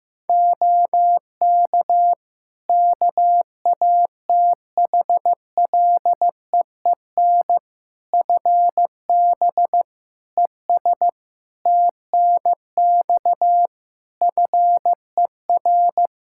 Morse code, heard as a keyed tone, about 15 wpm, 700 Hz. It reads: OK KATHLEEN FB ES TNX FER
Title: Practice QSO - Exchange 2 - Info - 15wpm